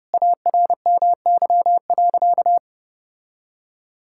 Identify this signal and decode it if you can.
Morse code, heard as a keyed tone, about 30 wpm, 700 Hz. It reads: ARMY.
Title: Sentences from Top 1000 Words - Review Part 5 - 30wpm